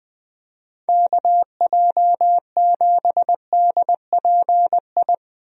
Morse code, heard as a keyed tone, about 20 wpm, 700 Hz. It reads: KJ7DPI
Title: USA Call Signs - 6 Characters - Repeated 3x - 20wpm